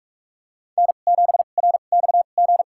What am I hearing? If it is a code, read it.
Morse code, heard as a keyed tone, about 43 wpm, spaced as for 32 wpm, 700 Hz. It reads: N7RXG
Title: USA Call Signs - 5 Characters - 2x Character Spacing - 45wpm